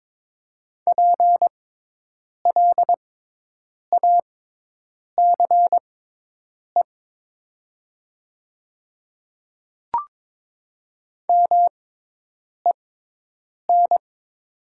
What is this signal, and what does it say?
Morse code, heard as a keyed tone, about 22 wpm, spaced as for 8 wpm, 700 Hz. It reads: PLACE MEN
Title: 6x Character Spacing - Top 200 Words - 22wpm